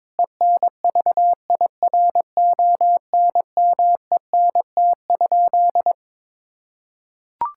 Morse code, heard as a keyed tone, about 22 wpm, 700 Hz. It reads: ENVIRONMENT?
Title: Sentences from Top 1000 Words - Review Part 5 - 22wpm